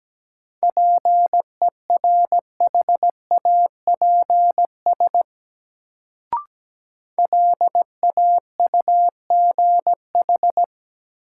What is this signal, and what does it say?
Morse code, heard as a keyed tone, about 17 wpm, 700 Hz. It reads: PERHAPS LAUGH
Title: Sentences from Top 500 Words - New Word Drill - 17wpm